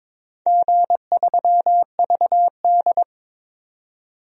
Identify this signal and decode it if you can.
Morse code, heard as a keyed tone, about 22 wpm, 700 Hz. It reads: G3VD
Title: DXCC-100 Call Signs - 4 Characters - 22wpm